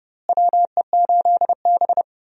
Morse code, heard as a keyed tone, about 30 wpm, 700 Hz. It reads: WE8B
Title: USA Call Signs - 4 Characters - Repeated 2x - 30wpm